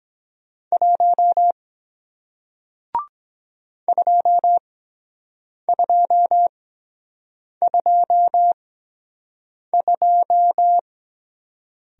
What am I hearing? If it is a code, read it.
Morse code, heard as a keyed tone, about 23 wpm, spaced as for 7 wpm, 700 Hz. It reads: 1 2222